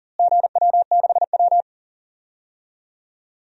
Morse code, heard as a keyed tone, about 40 wpm, 700 Hz. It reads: GW6W
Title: DXCC-100 Call Signs - 4 Characters - Repeated 2x - 40wpm